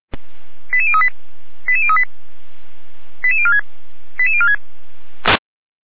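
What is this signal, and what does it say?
Five-tone selective call (ZVEI), heard as quick runs of five tones, tones of about 70 ms. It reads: 88038 88038 88046 88046
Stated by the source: ZVEI five-tone call 88159 received via SDR.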